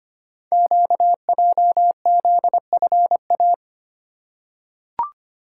Morse code, heard as a keyed tone, about 25 wpm, 700 Hz. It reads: QJZFA
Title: Code Groups - 5 Letters - 25wpm